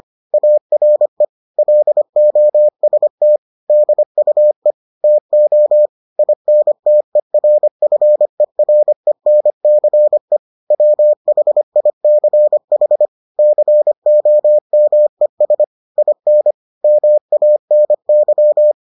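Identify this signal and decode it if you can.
Morse code, heard as a keyed tone, about 25 wpm, 600 Hz. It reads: ARE LOST DUE TO INTERFERENCE WHICH COMES IN MANY
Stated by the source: Morse edition of the Foundations of Amateur Radio episodes.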